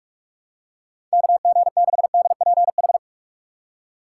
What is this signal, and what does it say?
Morse code, heard as a keyed tone, about 44 wpm, 700 Hz. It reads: KG6DPH
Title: USA Call Signs - 6 Characters - 6x Character Spacing - 45wpm